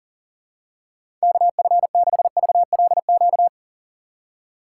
Morse code, heard as a keyed tone, about 37 wpm, 700 Hz. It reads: KF6VLQ
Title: USA Call Signs - 6 Characters - 3x Character Spacing - 40wpm